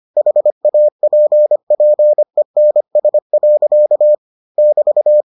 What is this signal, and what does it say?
Morse code, heard as a keyed tone, about 25 wpm, 600 Hz. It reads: HAPPENS. <BT>